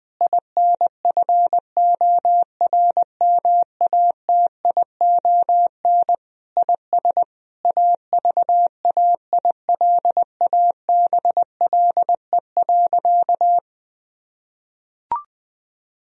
Morse code, heard as a keyed tone, about 21 wpm, 700 Hz. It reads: INFORMATION IS AVAILABLE.
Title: Sentences from Top 1000 Words - Review Part 3 - 20wpm